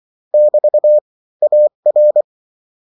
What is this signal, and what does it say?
Morse code, heard as a keyed tone, about 24 wpm, 600 Hz. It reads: <BT> AR
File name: morse mp3 / lecon16-vitesse 22